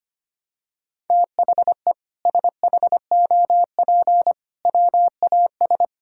Morse code, heard as a keyed tone, about 25 wpm, 700 Hz. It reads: THE SHOP WAS
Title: Sentences from Top 1600 Words 25wpm